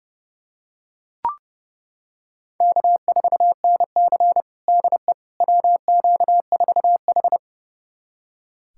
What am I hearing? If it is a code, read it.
Morse code, heard as a keyed tone, about 32 wpm, 700 Hz. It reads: K4NC DE WQ4H
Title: USA Callsign DE Callsign - 4 Characters - 30wpm